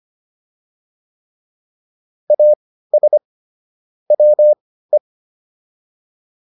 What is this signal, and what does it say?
Morse code, heard as a keyed tone, about 25 wpm, spaced as for 15 wpm, 600 Hz. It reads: AS WE